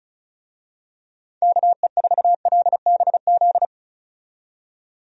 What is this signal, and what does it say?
Morse code, heard as a keyed tone, about 36 wpm, 700 Hz. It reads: KE4LBZ
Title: USA Call Signs - 6 Characters - 35wpm